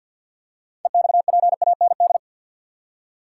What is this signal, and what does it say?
Morse code, heard as a keyed tone, about 52 wpm, 700 Hz. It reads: EXPAND
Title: Six Letter Words - 50wpm